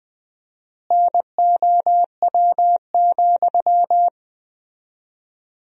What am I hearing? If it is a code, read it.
Morse code, heard as a keyed tone, about 20 wpm, 700 Hz. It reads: NOW,
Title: 4x Word-Spacing - Sentences from Top 300 Words - 20wpm